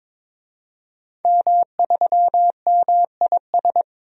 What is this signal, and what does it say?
Morse code, heard as a keyed tone, about 22 wpm, 700 Hz. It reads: M3MIS
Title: DXCC-100 Call Signs - 5 Characters - 22wpm